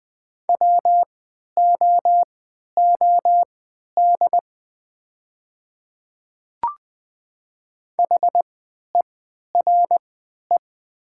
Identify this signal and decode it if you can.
Morse code, heard as a keyed tone, about 20 wpm, spaced as for 11 wpm, 700 Hz. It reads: WOOD HERE
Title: 3x Character Spacing - Top 500 Words - 20wpm